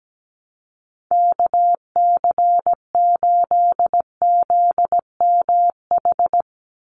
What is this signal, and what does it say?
Morse code, heard as a keyed tone, about 17 wpm, 700 Hz. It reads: KC8ZMH
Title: USA Call Signs 17wpm